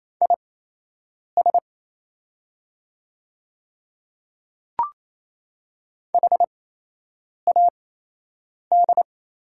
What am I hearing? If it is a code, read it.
Morse code, heard as a keyed tone, about 27 wpm, spaced as for 8 wpm, 690 Hz. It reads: IS HAD
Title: 8x Character Spacing - Top 300 Words - 28wpm